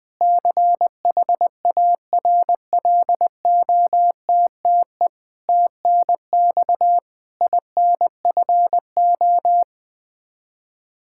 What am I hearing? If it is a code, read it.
Morse code, heard as a keyed tone, about 21 wpm, 700 Hz. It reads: CHARLOTTE TNX INFO
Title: Practice QSO - Exchange 3 - More Info - 20wpm